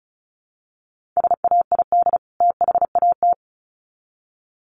Morse code, heard as a keyed tone, about 34 wpm, 700 Hz. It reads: SAID THAT
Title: Sets of 2 Words 35wpm